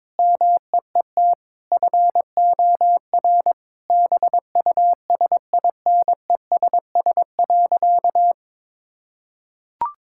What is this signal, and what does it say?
Morse code, heard as a keyed tone, about 22 wpm, 700 Hz. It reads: MEET FOR BUSINESS.
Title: Sentences from Top 1000 Words - Review Part 6 - 22wpm